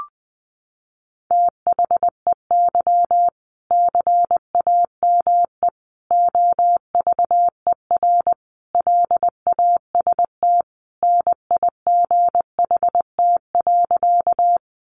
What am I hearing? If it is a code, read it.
Morse code, heard as a keyed tone, about 20 wpm, 700 Hz. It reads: THEY CAME OVER LAST NIGHT.
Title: Sentences from Top 300 Words 20wpm